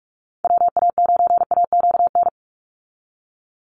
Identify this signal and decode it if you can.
Morse code, heard as a keyed tone, about 44 wpm, 700 Hz. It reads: WA9AQN
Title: Call Signs Repeated 2x - 45wpm